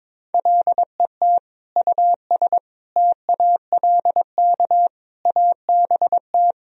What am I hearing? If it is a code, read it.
Morse code, heard as a keyed tone, about 22 wpm, 700 Hz. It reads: LET US TALK ABT
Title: Sentences from Top 400 Words and Abbreviations 22wpm